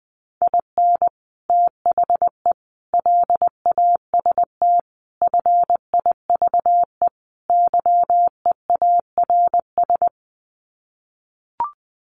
Morse code, heard as a keyed tone, about 19 wpm, 700 Hz. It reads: IN THE LAST FIVE YEARS
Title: Sets of 5 Words Encore 20wpm